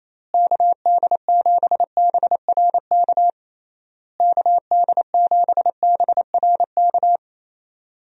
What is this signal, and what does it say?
Morse code, heard as a keyed tone, about 28 wpm, 700 Hz. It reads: KD7BRK KD7BRK
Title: USA Call Signs - 6 Characters - Repeated 3x - 28wpm